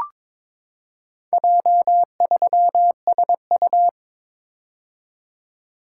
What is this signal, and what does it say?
Morse code, heard as a keyed tone, about 22 wpm, 700 Hz. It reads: J3SU
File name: practice / 22wpm / Four Letter-Number Combinations - 22wpm